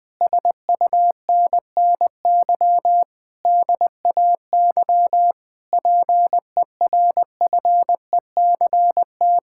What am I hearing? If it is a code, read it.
Morse code, heard as a keyed tone, about 21 wpm, 700 Hz. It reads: SUNNY DAY PERFECT
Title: Difficult Sentences with Hints - 20wpm